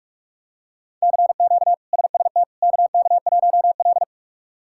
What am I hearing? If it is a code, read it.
Morse code, heard as a keyed tone, about 46 wpm, 700 Hz. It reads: CQ SST KK1L
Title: K1USN SST QSO Practice - 45wpm